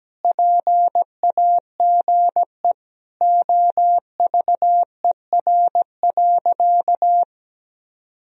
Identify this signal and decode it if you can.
Morse code, heard as a keyed tone, about 17 wpm, 700 Hz. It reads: PAGE OVER.
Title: Sentences from Top 1000 Words - Review Part 1 - 17wpm